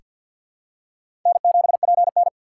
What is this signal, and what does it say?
Morse code, heard as a keyed tone, about 48 wpm, 700 Hz. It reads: N7PN
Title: USA Call Signs - 4 Characters - 2x Character Spacing - 50wpm